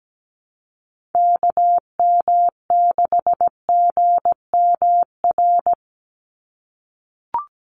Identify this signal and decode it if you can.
Morse code, heard as a keyed tone, about 17 wpm, 700 Hz. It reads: KM6GMR
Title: USA Call Signs 17wpm